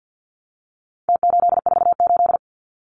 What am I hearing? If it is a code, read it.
Morse code, heard as a keyed tone, about 50 wpm, 700 Hz. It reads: T848
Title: Numbers - Contest Style - 4-Digit 50wpm